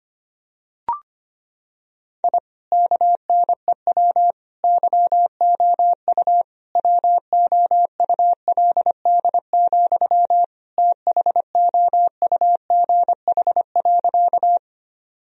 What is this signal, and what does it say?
Morse code, heard as a keyed tone, about 25 wpm, 700 Hz. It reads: I KNEW YOU WOULD, THOUGH.
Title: Sentences from Top 1000 Words - Review Part 2 - 25wpm